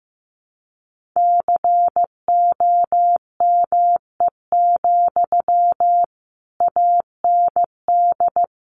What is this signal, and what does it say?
Morse code, heard as a keyed tone, about 15 wpm, 700 Hz. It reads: COME, AND
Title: Sentences from Top 500 Words 15wpm